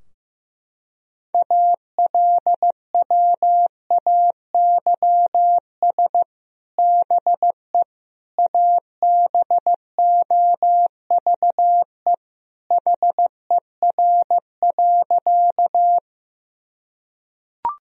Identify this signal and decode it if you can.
Morse code, heard as a keyed tone, about 15 wpm, 700 Hz. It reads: ALWAYS BE ABOVE HER.
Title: Sentences from Top 1000 Words - Review Part 2 - 15wpm